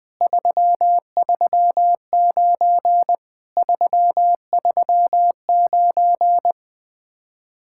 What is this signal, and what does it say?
Morse code, heard as a keyed tone, about 20 wpm, 700 Hz. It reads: 339 339
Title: Speed-Racing Common QSO Phrases - 17wpm